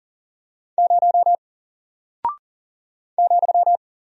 Morse code, heard as a keyed tone, about 40 wpm, 700 Hz. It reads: 0 ,
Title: Single Character (Letter:Number:Punctuation:Prosign) Alt 40wpm